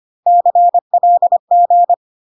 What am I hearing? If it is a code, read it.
Morse code, heard as a keyed tone, about 25 wpm, 700 Hz. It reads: CLG
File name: CW abbreviations alphabetical repeated 3 times 25 25wpm